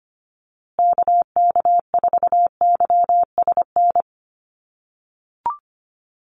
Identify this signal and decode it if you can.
Morse code, heard as a keyed tone, about 25 wpm, 700 Hz. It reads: KK4YSN
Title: USA Call Signs Repeated 2x 25wpm